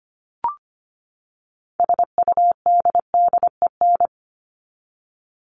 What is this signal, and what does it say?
Morse code, heard as a keyed tone, about 25 wpm, 700 Hz. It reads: SUDDEN
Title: LY Suffix - Top 100 Words with Base Words 25wpm